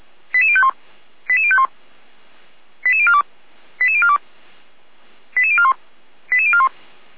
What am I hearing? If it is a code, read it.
Five-tone selective call (ZVEI), heard as quick runs of five tones, tones of about 70 ms. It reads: 88051 88051 88042 88042 88041 88041